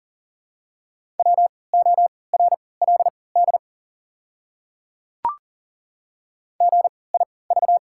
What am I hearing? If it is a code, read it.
Morse code, heard as a keyed tone, about 40 wpm, spaced as for 23 wpm, 690 Hz. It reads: WORLD GIV
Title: 3x Character Spacing - Top 200 Words - 40wpm